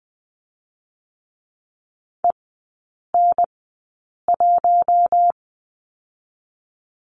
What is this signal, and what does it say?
Morse code, heard as a keyed tone, about 20 wpm, spaced as for 8 wpm, 690 Hz. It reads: EN1
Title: 205 Lesson 33e - Letter j - Three Characters 20wpm